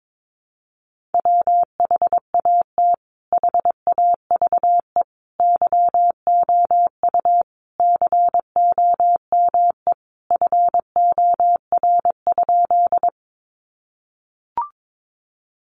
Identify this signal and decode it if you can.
Morse code, heard as a keyed tone, about 22 wpm, 700 Hz. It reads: WHAT HAVE YOU COME FOR?
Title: Sentences from Top 100 Words 22wpm